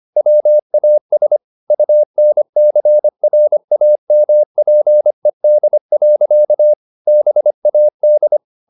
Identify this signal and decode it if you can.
Morse code, heard as a keyed tone, about 25 wpm, 600 Hz. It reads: WAS UNCRAMPED. BAD
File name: the old man and the sea part 1 25wpm